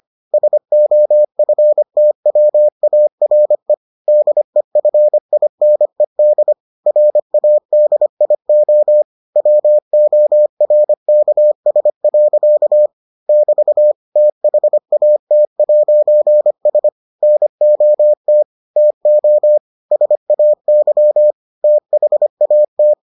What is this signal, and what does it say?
Morse code, heard as a keyed tone, about 25 wpm, 600 Hz. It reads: SOFTWARE DEFINED RADIO WORKS. <BT> THAT'S NOT TO SAY THAT